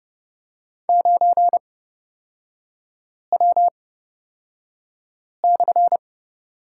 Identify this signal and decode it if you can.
Morse code, heard as a keyed tone, about 30 wpm, spaced as for 5 wpm, 700 Hz. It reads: 9W/
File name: Lesson 36e - BT - Three Characters 30wpm